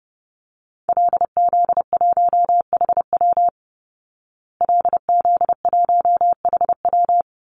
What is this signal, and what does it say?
Morse code, heard as a keyed tone, about 30 wpm, 700 Hz. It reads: LZ1HW LZ1HW
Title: Call Signs Repeated 3x - 30wpm